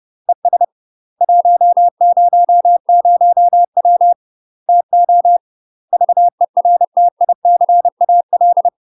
Morse code, heard as a keyed tone, about 30 wpm, 700 Hz. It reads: ES 100W TO VERTICAL
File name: UK MASTER QSOs TRAINING / 30WPM / UK MASTER QSOs 0027